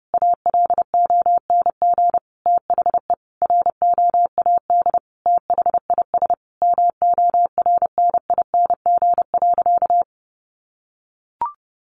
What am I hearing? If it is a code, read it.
Morse code, heard as a keyed tone, about 30 wpm, 700 Hz. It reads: ALONG THE ROAD THIS MORNING.
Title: Sentences from Top 1100 Words 30wpm